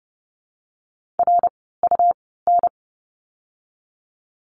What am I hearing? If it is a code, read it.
Morse code, heard as a keyed tone, about 30 wpm, spaced as for 17 wpm, 700 Hz. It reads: RUN
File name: course / 30:10wpm / 108 Lesson 18f - Letter m - Words 30wpm